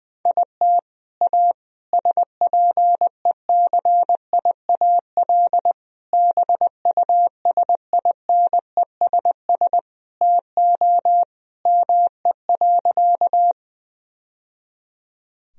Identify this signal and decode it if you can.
Morse code, heard as a keyed tone, about 20 wpm, 700 Hz. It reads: IT A SPECIAL BUSINESS TO ME.
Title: Sentences from Top 1000 Words - Review Part 3 - 20wpm